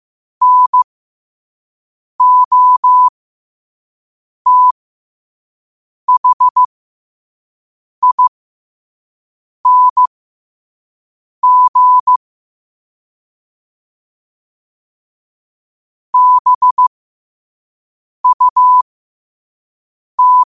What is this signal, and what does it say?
Morse code, heard as a keyed tone, about 15 wpm, spaced as for 5 wpm, 1000 Hz. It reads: NOTHING BUT
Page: New quotes every day in morse code at 5 Words per minute.